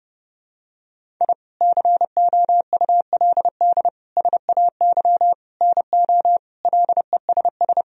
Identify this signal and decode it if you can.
Morse code, heard as a keyed tone, about 32 wpm, 700 Hz. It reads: I COULD SAY NO LESS
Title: Sentences from Top 1000 Words - Review Part 2 - 30wpm